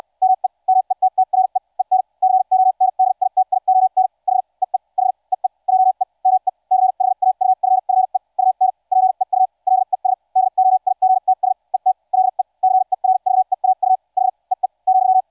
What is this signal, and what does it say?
Morse code, heard as a keyed tone, about 74 wpm, spaced as for 52 wpm, 750 Hz. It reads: ME METTRE EN CONDITION H EE H EE JE ME CHAUFFE UN PEU LES DOIGTS ET LE CERVEAU H EE ,